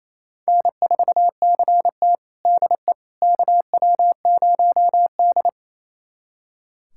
Morse code, heard as a keyed tone, about 27 wpm, 690 Hz. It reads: N4CT DE KW0D
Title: USA Callsign DE Callsign - 4 Characters - 28wpm